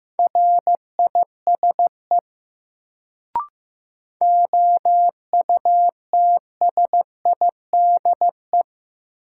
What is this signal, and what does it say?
Morse code, heard as a keyed tone, about 15 wpm, 700 Hz. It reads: RISE OUTSIDE
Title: Sentences from Top 900 Words - New Word Drill - 15wpm